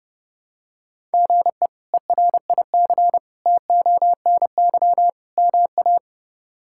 Morse code, heard as a keyed tone, about 32 wpm, 700 Hz. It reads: GE ERIC TONY MA
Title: K1USN SST QSO Practice - 30wpm